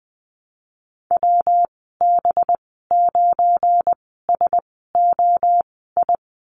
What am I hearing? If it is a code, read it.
Morse code, heard as a keyed tone, about 20 wpm, spaced as for 15 wpm, 700 Hz. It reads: WB9SOI
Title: 186 Lesson 30g - Letter b - Call Signs 20wpm